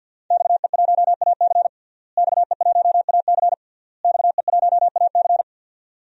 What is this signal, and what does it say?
Morse code, heard as a keyed tone, about 51 wpm, 700 Hz. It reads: XE1AC XE1AC XE1AC
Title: DXCC-100 Call Signs - 5 Characters - Repeated 3x - 50wpm